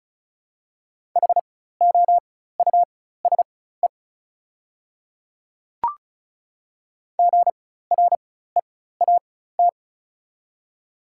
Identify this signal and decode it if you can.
Morse code, heard as a keyed tone, about 36 wpm, spaced as for 16 wpm, 690 Hz. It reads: HOUSE GREAT
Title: 4x Character Spacing - Top 200 Words - 35wpm